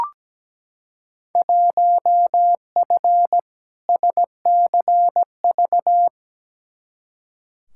Morse code, heard as a keyed tone, about 17 wpm, 700 Hz. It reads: 1F SCV
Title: Field Day Exchanges Encore - 17wpm